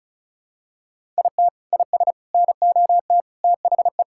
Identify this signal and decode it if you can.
Morse code, heard as a keyed tone, about 36 wpm, 700 Hz. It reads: IT IS NOT THE
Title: Sentences from Top 1000 Words - Review Part 5 - 35wpm